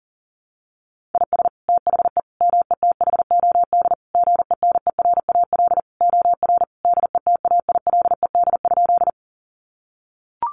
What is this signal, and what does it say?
Morse code, heard as a keyed tone, about 40 wpm, 700 Hz. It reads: IS THE METHOD GENERAL OR DETAILED?
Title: Sentences from Top 1500 Words 40wpm